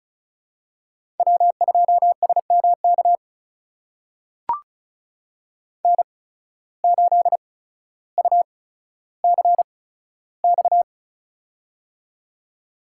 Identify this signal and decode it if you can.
Morse code, heard as a keyed tone, about 36 wpm, spaced as for 15 wpm, 700 Hz. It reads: W2SMK N8UCX
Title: USA Call Signs - 5 Characters - 8x Character Spacing - 35wpm